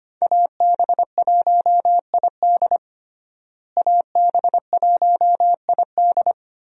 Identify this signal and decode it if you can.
Morse code, heard as a keyed tone, about 25 wpm, 700 Hz. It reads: AB1ID AB1ID
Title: USA Call Signs - 5 Characters - Repeated 3x - 25wpm